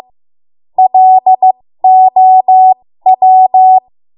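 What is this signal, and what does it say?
Morse code, heard as a keyed tone, about 15 wpm, spaced as for 13 wpm, 750 Hz. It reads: LOW
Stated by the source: ARRL Morse code practice at 13 WPM.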